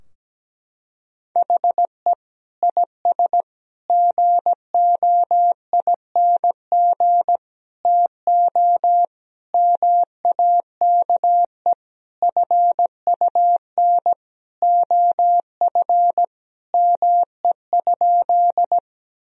Morse code, heard as a keyed tone, about 17 wpm, 700 Hz. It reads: HE IS GOING TO MAKE FUN OF ME?
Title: Sentences from Top 1000 Words - Review Part 4 - 17wpm